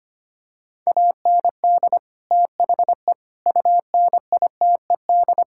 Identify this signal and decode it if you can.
Morse code, heard as a keyed tone, about 25 wpm, 700 Hz. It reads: AND THE UNITED
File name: practice / 25wpm / Sentences with 4-Word Phrases - 25wpm